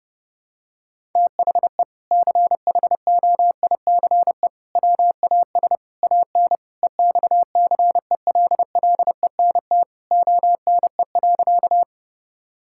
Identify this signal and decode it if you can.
Morse code, heard as a keyed tone, about 30 wpm, 700 Hz. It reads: THE CHOICE WAS AN EXCELLENT ONE.
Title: Sentences from Top 1000 Words - Review Part 6 - 30wpm